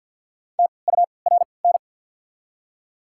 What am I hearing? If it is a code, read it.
Morse code, heard as a keyed tone, about 53 wpm, spaced as for 29 wpm, 700 Hz. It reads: TURN
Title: 3x Character Spacing - Top 500 Words - 50wpm